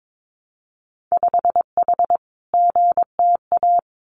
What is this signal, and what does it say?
Morse code, heard as a keyed tone, about 22 wpm, 700 Hz. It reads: 5H GTA
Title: Winter Field Day Exchanges Encore 22wpm